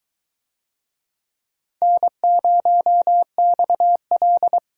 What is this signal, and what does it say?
Morse code, heard as a keyed tone, about 20 wpm, 700 Hz. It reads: N0XL